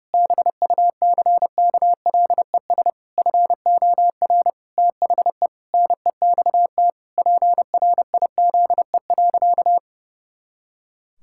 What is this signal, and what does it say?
Morse code, heard as a keyed tone, about 32 wpm, 700 Hz. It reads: BUCKLES FOR THE NEXT PRIZE.
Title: Pangram Challenge v2 - 30wpm